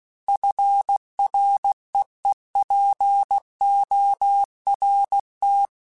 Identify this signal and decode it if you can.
Morse code, heard as a keyed tone, about 16 wpm, 800 Hz. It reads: FREEPORT
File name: fromSMs / FromSMs mp3s / Cities@15